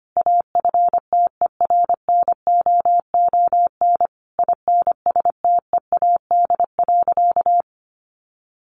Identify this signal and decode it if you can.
Morse code, heard as a keyed tone, about 25 wpm, 700 Hz. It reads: AFTERNOON INSTEAD.
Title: Sentences from Top 1500 Words 25wpm